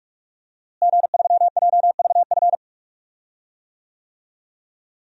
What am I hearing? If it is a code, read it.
Morse code, heard as a keyed tone, about 42 wpm, 700 Hz. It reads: G3JVF
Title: DXCC-100 Call Signs - 5 Characters - Repeated 2x - 45wpm